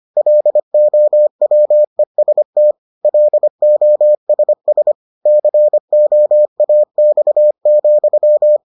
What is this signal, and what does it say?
Morse code, heard as a keyed tone, about 25 wpm, 600 Hz. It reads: LOWEST LOSS COAX,